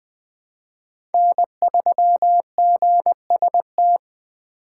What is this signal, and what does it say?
Morse code, heard as a keyed tone, about 20 wpm, 700 Hz. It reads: N3GST
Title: USA Call Signs - 4x Character Spacing - 20wpm